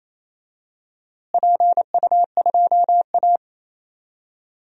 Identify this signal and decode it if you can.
Morse code, heard as a keyed tone, about 28 wpm, 700 Hz. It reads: PU2A
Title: DXCC-100 Call Signs - 4 Characters - Repeated 2x - 28wpm